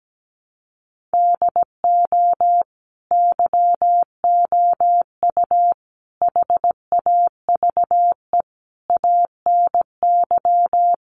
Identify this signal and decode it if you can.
Morse code, heard as a keyed tone, about 17 wpm, 700 Hz. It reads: DO YOU HAVE ANY
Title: Sentences from Top 1100 Words 17wpm